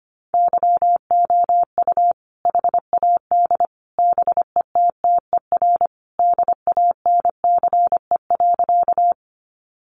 Morse code, heard as a keyed tone, about 25 wpm, 700 Hz. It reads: YOU HAD BETTER DANCE.
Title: Sentences from Top 700 Words 25wpm